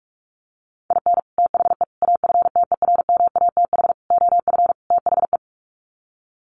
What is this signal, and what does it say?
Morse code, heard as a keyed tone, about 44 wpm, 700 Hz. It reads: IN THE AFTERMATH OF THE
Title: Sets of 5 Words 45wpm